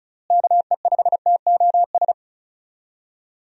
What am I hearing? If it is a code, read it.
Morse code, heard as a keyed tone, about 36 wpm, 700 Hz. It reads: KE5TOS
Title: USA Call Signs - 6 Characters - Repeated 2x - 35wpm